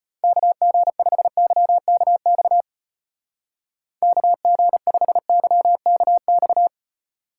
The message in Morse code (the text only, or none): KG5YKX KG5YKX